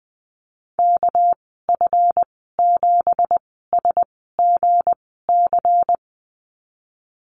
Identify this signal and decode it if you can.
Morse code, heard as a keyed tone, about 20 wpm, spaced as for 15 wpm, 700 Hz. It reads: KF7SGC
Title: 207 Lesson 33g - Letter j - Call Signs 20wpm